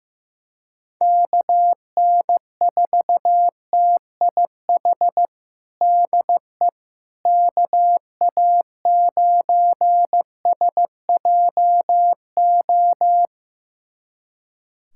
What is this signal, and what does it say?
Morse code, heard as a keyed tone, about 15 wpm, 700 Hz. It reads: KN4TIH DE KA9SJO
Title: USA Callsign DE Callsign - 6 Characters - 15wpm